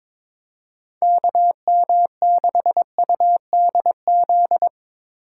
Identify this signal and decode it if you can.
Morse code, heard as a keyed tone, about 22 wpm, 700 Hz. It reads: KM6UDZ
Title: USA Call Signs - 8x Character Spacing - 22wpm